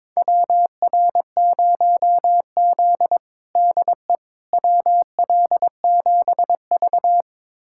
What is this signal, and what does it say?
Morse code, heard as a keyed tone, about 21 wpm, 700 Hz. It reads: WR0Z DE WL7V
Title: USA Callsign DE Callsign - 4 Characters - 22wpm